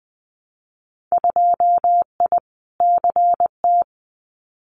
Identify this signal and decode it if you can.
Morse code, heard as a keyed tone, about 20 wpm, 700 Hz. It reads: 2I CT
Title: Winter Field Day Exchanges 20wpm